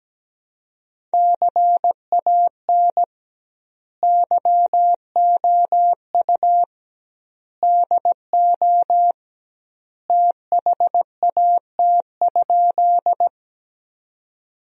Morse code, heard as a keyed tone, about 17 wpm, 700 Hz. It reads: CAN YOU DO THAT?
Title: 2x Word-Spacing - Sentences from Top 100 Words - 17wpm